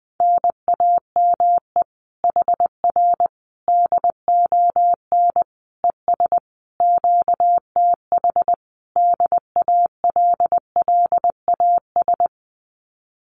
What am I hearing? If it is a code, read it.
Morse code, heard as a keyed tone, about 19 wpm, 700 Hz. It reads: NAME HR DON ES QTH DALLAS
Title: Common QSO Phrases Encore 20wpm